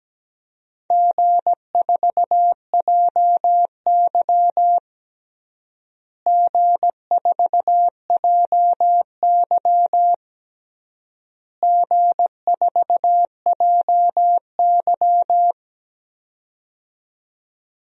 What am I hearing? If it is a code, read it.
Morse code, heard as a keyed tone, about 17 wpm, 700 Hz. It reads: G4JY G4JY G4JY
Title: DXCC-100 Call Signs - 4 Characters - Repeated 3x - 17wpm